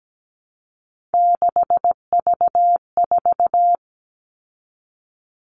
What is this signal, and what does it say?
Morse code, heard as a keyed tone, about 17 wpm, 700 Hz. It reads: 6V4
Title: All Cut Numbers - 3-Digit 17wpm